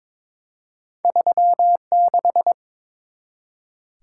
Morse code, heard as a keyed tone, about 21 wpm, 700 Hz. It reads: 36
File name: practice / 22wpm / Two Letter-Number Combinations - 22wpm